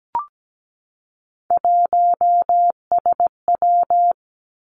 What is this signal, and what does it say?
Morse code, heard as a keyed tone, about 17 wpm, 700 Hz. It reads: 1SW
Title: Three Challenging Letter-Number Combinations 17wpm